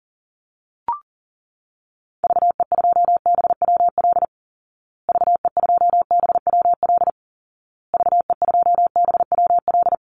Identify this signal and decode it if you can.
Morse code, heard as a keyed tone, about 40 wpm, 700 Hz. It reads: VE2BWL VE2BWL VE2BWL
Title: Call Signs Repeated 2x - 40wpm